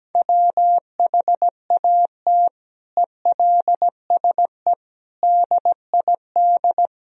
Morse code, heard as a keyed tone, about 17 wpm, 700 Hz. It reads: WHAT ELSE DID
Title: Sentences from Top 1000 Words - Review Part 1 - 17wpm